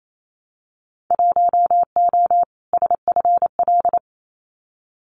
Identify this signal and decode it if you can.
Morse code, heard as a keyed tone, about 28 wpm, 700 Hz. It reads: 1O SFL
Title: Winter Field Day Exchanges 28wpm